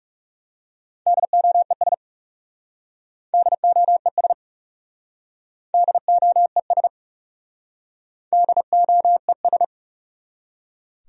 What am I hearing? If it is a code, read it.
Morse code, heard as a keyed tone, about 40 wpm, 700 Hz. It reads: DOES DOES DOES DOES